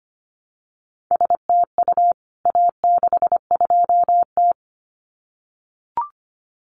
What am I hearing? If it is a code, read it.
Morse code, heard as a keyed tone, about 25 wpm, 700 Hz. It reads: STU A62T
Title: CWT Exchange 25wpm